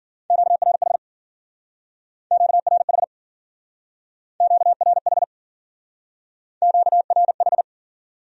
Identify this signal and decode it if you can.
Morse code, heard as a keyed tone, about 50 wpm, 700 Hz. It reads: QRH QRH QRH QRH